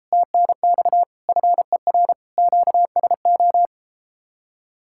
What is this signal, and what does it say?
Morse code, heard as a keyed tone, about 33 wpm, 700 Hz. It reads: TNX FER QSO